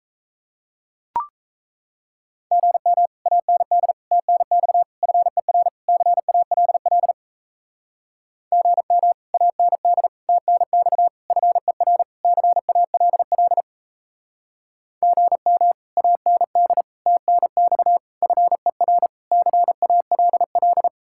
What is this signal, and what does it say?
Morse code, heard as a keyed tone, about 37 wpm, 700 Hz. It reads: GM AND TNX FER CALL GM AND TNX FER CALL GM AND TNX FER CALL